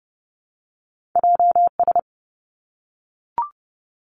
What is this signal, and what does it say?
Morse code, heard as a keyed tone, about 30 wpm, 700 Hz. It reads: J S
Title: Two Challenging Letter-Number Combinations 30wpm